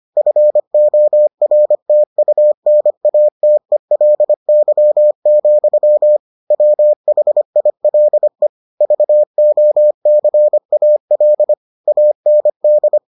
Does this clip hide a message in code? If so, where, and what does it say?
Yes: Morse code from 0.2 to 13.0 s, heard as a keyed tone, about 25 wpm, 600 Hz. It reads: FORTUNATELY, WHILE VOCAL AND